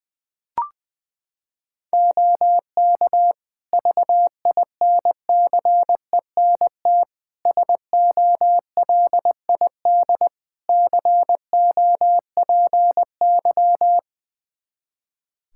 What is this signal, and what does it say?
Morse code, heard as a keyed tone, about 20 wpm, 700 Hz. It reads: OK VINCENT SOLID COPY
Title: Practice QSO - Exchange 3 - More Info - 20wpm